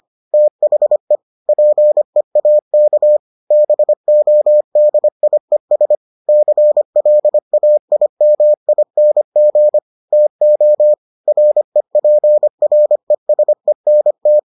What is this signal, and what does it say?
Morse code, heard as a keyed tone, about 25 wpm, 600 Hz. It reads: THE PEAK BODIES CLAIMING TO REPRESENT